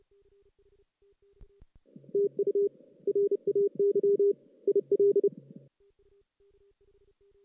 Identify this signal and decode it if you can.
Morse code, heard as a keyed tone, about 30 wpm, 400 Hz. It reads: TU RAY IL
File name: 2016 JAN CW NAQP / 2016 Jan naqp k5ot - 0-5